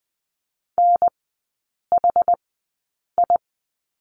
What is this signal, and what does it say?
Morse code, heard as a keyed tone, about 20 wpm, spaced as for 8 wpm, 700 Hz. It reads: NHI
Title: 060 Lesson 11e - Letter h - Three Characters 20wpm